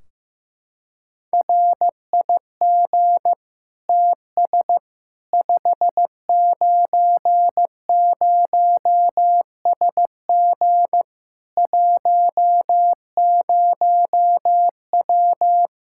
Morse code, heard as a keyed tone, about 15 wpm, 700 Hz. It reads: RIG TS 590SG 10W
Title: Practice QSO - Exchange 2 - Info - 15wpm